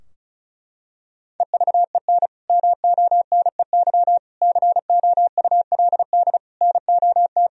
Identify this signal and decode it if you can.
Morse code, heard as a keyed tone, about 36 wpm, 700 Hz. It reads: EVEN MONEY COULD NOT
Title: Sentences from Top 1000 Words - Review Part 4 - 35wpm